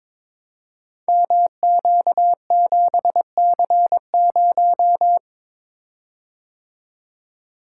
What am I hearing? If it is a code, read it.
Morse code, heard as a keyed tone, about 22 wpm, 700 Hz. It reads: MQ7C0
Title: Code Groups - 5 Letter:Numbers - 22wpm